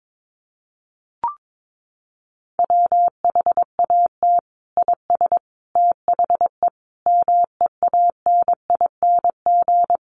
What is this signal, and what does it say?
Morse code, heard as a keyed tone, about 22 wpm, 700 Hz. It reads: WHAT IS THE MEANING
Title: Sentences from Top 900 Words 22wpm